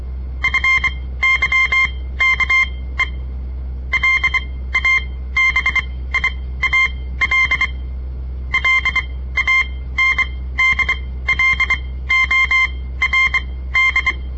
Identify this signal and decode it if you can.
Morse code, heard as a keyed tone, about 23 wpm, spaced as for 15 wpm, 2100 Hz. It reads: FYKE LABIAL LANDLORD
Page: On the air Morse code practice from the Southern Montana Amatuer Radio Association; all code, no read back